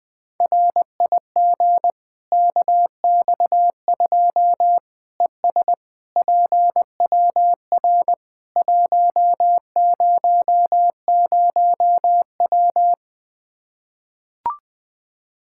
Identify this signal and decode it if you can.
Morse code, heard as a keyed tone, about 20 wpm, 700 Hz. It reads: RIG KX2 ES PWR 100W
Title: Practice QSO - Exchange 2 - Info - 20wpm